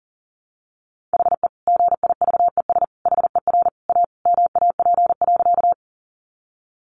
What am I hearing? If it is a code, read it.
Morse code, heard as a keyed tone, about 38 wpm, 700 Hz. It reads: HE GIVES HER A MAP.
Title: Additional Easy Phrases 40wpm